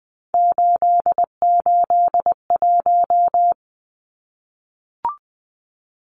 Morse code, heard as a keyed tone, about 20 wpm, 700 Hz. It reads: 881
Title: Numbers - 3-Digit 20wpm